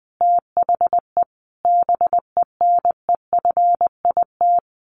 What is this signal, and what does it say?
Morse code, heard as a keyed tone, about 20 wpm, 700 Hz. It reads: THE BENEFIT
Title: Sentences from Top 1300 Words 20wpm